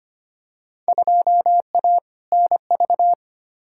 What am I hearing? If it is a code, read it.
Morse code, heard as a keyed tone, about 25 wpm, 700 Hz. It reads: 2A NV
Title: Field Day Exchanges - 25wpm